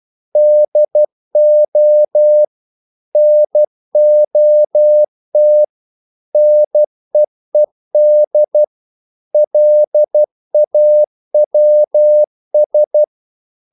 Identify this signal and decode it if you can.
Morse code, heard as a keyed tone, about 12 wpm, 600 Hz. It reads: DO NOT NEED LAWS
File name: CW / Practice / EnglishQuotations0112